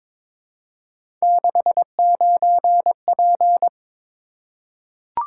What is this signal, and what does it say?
Morse code, heard as a keyed tone, about 22 wpm, 700 Hz. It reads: 69P
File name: practice / 22wpm / Three Letter-Number Combinations - 22wpm